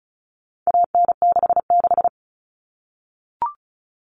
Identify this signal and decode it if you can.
Morse code, heard as a keyed tone, about 34 wpm, 700 Hz. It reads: AN66
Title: All Cut Numbers - 4-Digit 35wpm